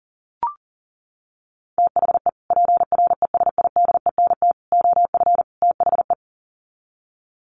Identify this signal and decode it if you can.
Morse code, heard as a keyed tone, about 40 wpm, 700 Hz. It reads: THE PRESIDENT OF THE
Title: Sets of 4 Words 40wpm